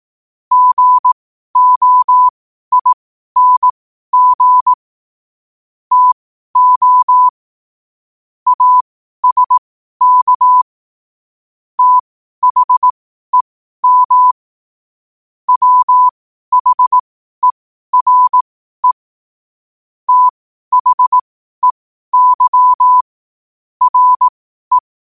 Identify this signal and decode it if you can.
Morse code, heard as a keyed tone, about 18 wpm, spaced as for 12 wpm, 1000 Hz. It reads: GOING TO ASK THEM WHERE THEY RE
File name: qotdfiles / 12wpm